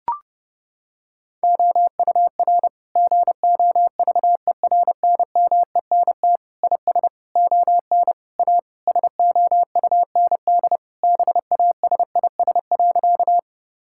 Morse code, heard as a keyed tone, about 30 wpm, 700 Hz. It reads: OUR GOVERNMENT IS ON A SOUND BASIS.
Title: Sentences from Top 1000 Words - Review Part 5 - 30wpm